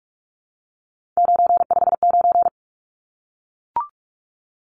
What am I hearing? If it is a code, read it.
Morse code, heard as a keyed tone, about 44 wpm, 700 Hz. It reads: 959
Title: Numbers - 3-Digit 45wpm